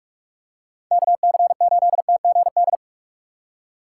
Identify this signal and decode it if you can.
Morse code, heard as a keyed tone, about 44 wpm, 700 Hz. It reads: KC8TGD